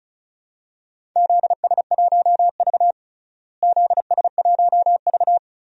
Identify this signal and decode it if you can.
Morse code, heard as a keyed tone, about 34 wpm, 700 Hz. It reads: ZS1V ZS1V
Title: DXCC-100 Call Signs - 4 Characters - Repeated 3x - 35wpm